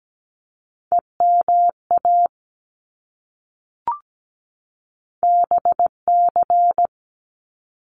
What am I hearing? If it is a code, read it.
Morse code, heard as a keyed tone, about 17 wpm, 700 Hz. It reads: EMA BC
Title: ARRL Sections 17wpm